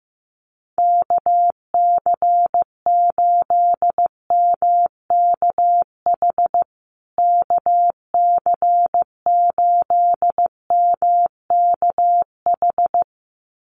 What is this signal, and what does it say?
Morse code, heard as a keyed tone, about 15 wpm, 700 Hz. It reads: KC8MKH KC8MKH
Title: USA Call Signs Repeated 2x 15wpm